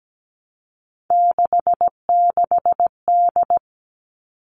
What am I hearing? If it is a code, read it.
Morse code, heard as a keyed tone, about 17 wpm, 700 Hz. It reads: 66D
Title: All Cut Numbers - 3-Digit 17wpm